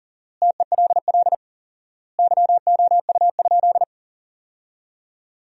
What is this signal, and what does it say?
Morse code, heard as a keyed tone, about 40 wpm, 700 Hz. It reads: TELL YOU?
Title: 4x Word-Spacing - Sentences from Top 200 Words - 40wpm